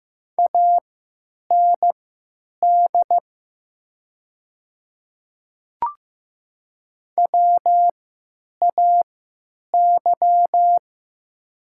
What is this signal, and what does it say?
Morse code, heard as a keyed tone, about 15 wpm, spaced as for 9 wpm, 700 Hz. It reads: AND WAY
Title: 3x Character Spacing - Top 100 Words - 15wpm